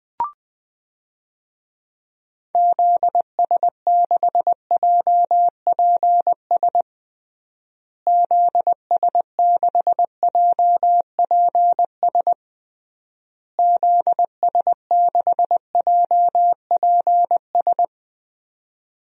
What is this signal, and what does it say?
Morse code, heard as a keyed tone, about 20 wpm, 700 Hz. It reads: ZS6JPS ZS6JPS ZS6JPS
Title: DXCC-100 Call Signs - 6 Characters - Repeated 3x - 20wpm